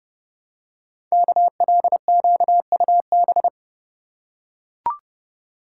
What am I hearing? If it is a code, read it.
Morse code, heard as a keyed tone, about 30 wpm, 700 Hz. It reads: KLQUB
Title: Code Groups - 5 Letter:Numbers - 30wpm